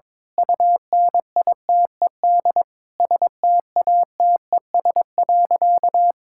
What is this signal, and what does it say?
Morse code, heard as a keyed tone, about 22 wpm, 700 Hz. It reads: UNITED STATES.
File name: Sentences with 9-Word Phrases - 22wpm